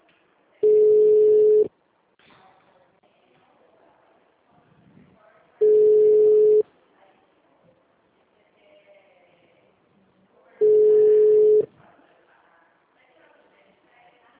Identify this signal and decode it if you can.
Ringback tone (Europe, 425 Hz), with speech.